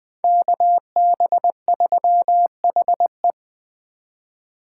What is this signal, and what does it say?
Morse code, heard as a keyed tone, about 24 wpm, 700 Hz. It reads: KB3HE
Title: Speed Racing - USA 2x2 Call Signs 20wpm